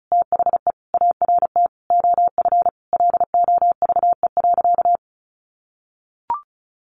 Morse code, heard as a keyed tone, about 34 wpm, 700 Hz. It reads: THE ART OF LOVE.
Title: Sentences from Top 1500 Words 35wpm